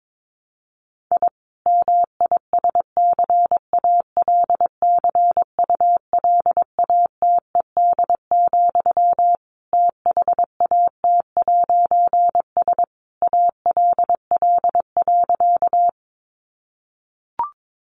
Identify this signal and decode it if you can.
Morse code, heard as a keyed tone, about 22 wpm, 700 Hz. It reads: I MISCALCULATED, THAT'S ALL.
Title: A Sound Of Thunder 22wpm